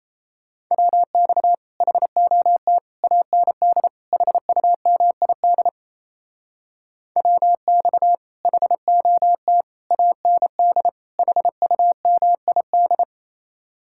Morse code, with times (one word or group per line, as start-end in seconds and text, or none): 0.7-1.5 WX
1.8-2.8 HOT
3.0-3.9 AND
4.1-5.7 HUMID
7.2-8.2 WX
8.4-9.6 HOT
9.9-10.9 AND
11.2-13.0 HUMID